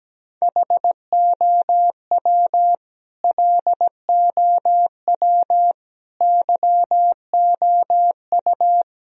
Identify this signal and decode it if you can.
Morse code, heard as a keyed tone, about 17 wpm, 700 Hz. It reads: HOW LOW YOU
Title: Sentences from Top 1000 Words - Review Part 1 - 17wpm